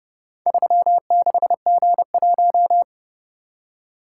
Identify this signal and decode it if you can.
Morse code, heard as a keyed tone, about 30 wpm, 700 Hz. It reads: 36G1